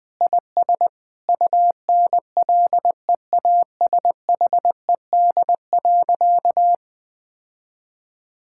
Morse code, heard as a keyed tone, about 21 wpm, 700 Hz. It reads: IS UNLEASHED.
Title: Difficult Sentences with Hints - 20wpm